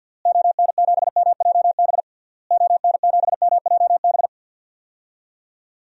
Morse code, heard as a keyed tone, about 48 wpm, 700 Hz. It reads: ON7MJB ON7MJB
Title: DXCC-100 Call Signs - 6 Characters - Repeated 3x - 50wpm